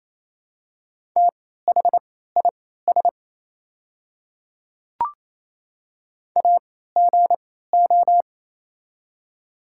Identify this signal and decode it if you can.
Morse code, heard as a keyed tone, about 27 wpm, spaced as for 16 wpm, 700 Hz. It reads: THIS AGO
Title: 3x Character Spacing - Top 500 Words - 28wpm